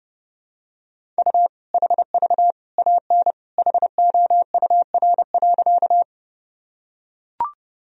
Morse code, heard as a keyed tone, about 30 wpm, 700 Hz. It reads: U HV AN HOUR.
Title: Sentences from Top 400 Words and Abbreviations 30wpm